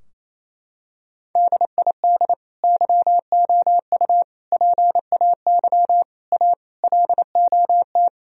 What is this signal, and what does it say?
Morse code, heard as a keyed tone, about 28 wpm, 700 Hz. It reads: DID YOU PAY A LOT
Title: Sentences from Top 1000 Words - Review Part 4 - 28wpm